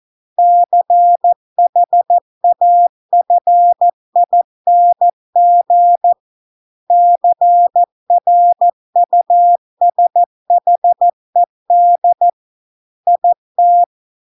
Morse code, heard as a keyed tone, about 14 wpm, 700 Hz. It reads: CHAFING CRUSHED IT